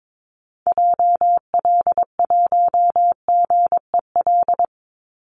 Morse code, heard as a keyed tone, about 22 wpm, 700 Hz. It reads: JL1GEL
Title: Call Signs Repeated 3x - 22wpm